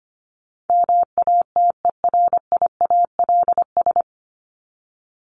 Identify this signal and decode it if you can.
Morse code, heard as a keyed tone, about 25 wpm, 700 Hz. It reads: MATERIALS
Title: ES:S Suffix - Top 100 Words with Base Words 25wpm